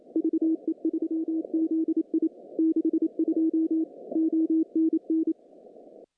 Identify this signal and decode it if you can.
Morse code, heard as a keyed tone, about 28 wpm, 320 Hz. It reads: VE3ZI 62 ONN
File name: contests / CW-SS-k5tr- -0000